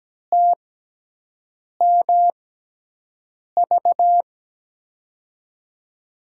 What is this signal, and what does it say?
Morse code, heard as a keyed tone, about 17 wpm, spaced as for 6 wpm, 700 Hz. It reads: TMV
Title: USA Call Signs - 5 Characters - 6x Character Spacing - 17wpm